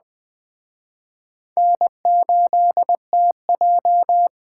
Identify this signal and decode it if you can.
Morse code, heard as a keyed tone, about 21 wpm, 700 Hz. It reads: N8TJ